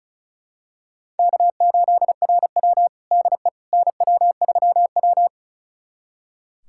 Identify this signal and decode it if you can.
Morse code, heard as a keyed tone, about 34 wpm, 700 Hz. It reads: K8RW DE NW3W
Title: USA Callsign DE Callsign - 4 Characters - 35wpm